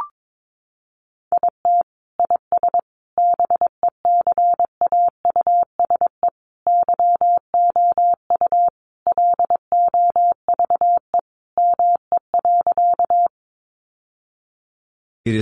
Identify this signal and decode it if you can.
Morse code, heard as a keyed tone, about 22 wpm, 700 Hz. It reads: IT IS BECAUSE YOU LOVE ME.
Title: Sentences from Top 600 Words 22wpm